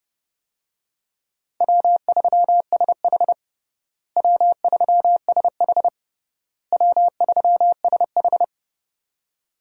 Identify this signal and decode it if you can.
Morse code, heard as a keyed tone, about 30 wpm, 700 Hz. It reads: W3SH W3SH W3SH
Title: USA Call Signs - 4 Characters - Repeated 3x - 30wpm